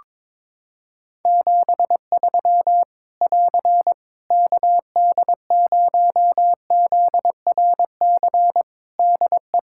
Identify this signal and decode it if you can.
Morse code, heard as a keyed tone, about 22 wpm, 700 Hz. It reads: 73 <AR> KD0ZRC DE
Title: Practice QSO - Exchange 4 - Ending - 22wpm